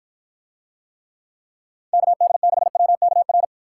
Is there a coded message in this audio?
Yes: Morse code from 1.9 to 3.4 s, heard as a keyed tone, about 52 wpm, 700 Hz. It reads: KD6WKR